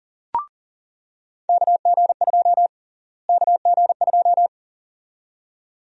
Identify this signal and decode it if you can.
Morse code, heard as a keyed tone, about 40 wpm, 700 Hz. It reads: KG2 KG2
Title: USA Call Sign Prefixes - Repeated 2x - 40wpm